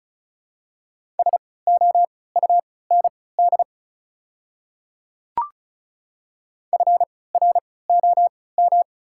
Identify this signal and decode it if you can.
Morse code, heard as a keyed tone, about 36 wpm, spaced as for 20 wpm, 700 Hz. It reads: SOUND FROM
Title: 3x Character Spacing - Top 200 Words - 35wpm